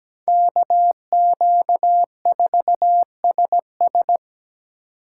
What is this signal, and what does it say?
Morse code, heard as a keyed tone, about 20 wpm, 700 Hz. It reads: KQ4SS
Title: Speed Racing - USA 2x2 Call Signs 17wpm